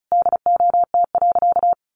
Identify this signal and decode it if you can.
Morse code, heard as a keyed tone, about 34 wpm, 700 Hz. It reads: DOT.
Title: Sentences from Top 1500 Words 35wpm